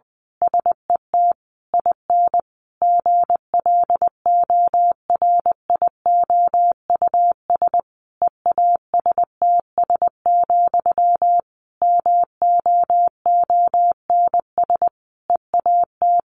Morse code, heard as a keyed tone, about 20 wpm, 700 Hz. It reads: SET IN GLORIOUS EASTS, MOONS EAT
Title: A Sound Of Thunder 20wpm